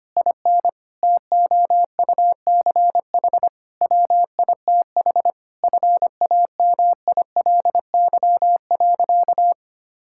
Morse code, heard as a keyed tone, about 25 wpm, 700 Hz. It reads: IN TOUCH WITH FAMILY.
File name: practice / 25wpm / Difficult Sentences with Hints - 25wpm